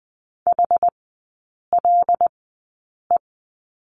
Morse code, heard as a keyed tone, about 20 wpm, spaced as for 8 wpm, 700 Hz. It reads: HLE
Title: 074 Lesson 13e - Letter L - Three Characters 20wpm